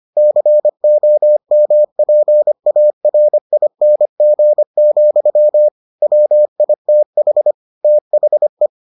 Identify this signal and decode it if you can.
Morse code, heard as a keyed tone, about 25 wpm, 600 Hz. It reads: COMPARING, WITH THE